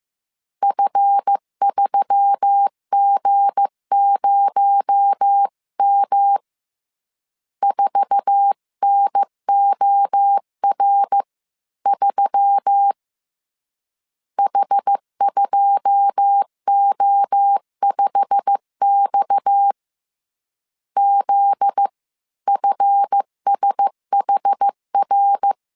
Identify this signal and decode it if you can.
Morse code, heard as a keyed tone, about 15 wpm, 800 Hz. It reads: F3G0M 4NOR3 H2O5X ZFSHR
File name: CW / DET / 1 9 1500C S A 14wpm 3222